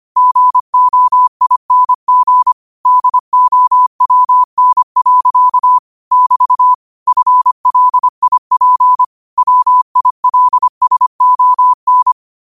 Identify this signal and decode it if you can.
Morse code, heard as a keyed tone, about 25 wpm, 1000 Hz. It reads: GOING DOWN. <BT> FLIP WILSON